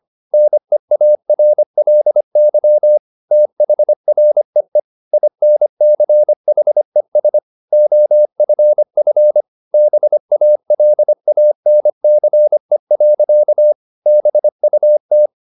Morse code, heard as a keyed tone, about 25 wpm, 600 Hz. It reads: NEARLY THREE INCHES OFF BALANCE. BUT